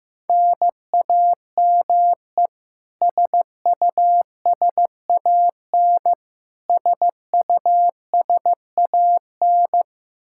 Morse code, heard as a keyed tone, about 15 wpm, 700 Hz. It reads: NAME SUSAN SUSAN
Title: Practice QSO - Exchange 1 - Beginning - 15wpm